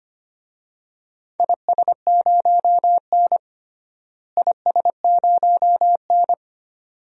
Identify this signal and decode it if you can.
Morse code, heard as a keyed tone, about 25 wpm, 700 Hz. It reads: IS0N IS0N
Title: DXCC-100 Call Signs - 4 Characters - Repeated 2x - 25wpm